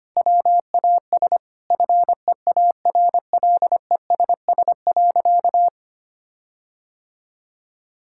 Morse code, heard as a keyed tone, about 25 wpm, 700 Hz. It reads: WAS FEARLESS.
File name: practice / 25wpm / Sentences with 4-Word Phrases - 25wpm